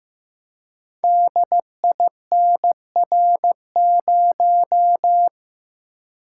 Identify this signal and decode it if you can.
Morse code, heard as a keyed tone, about 15 wpm, 700 Hz. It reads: DINR0
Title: Code Groups - 5 Letter:Numbers - 15wpm